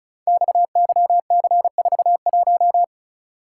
Morse code, heard as a keyed tone, about 36 wpm, 690 Hz. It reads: XYC41
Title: Code Groups - 5 Letter:Numbers - 35wpm